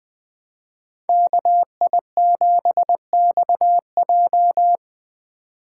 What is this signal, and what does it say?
Morse code, heard as a keyed tone, about 19 wpm, 700 Hz. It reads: KI7XJ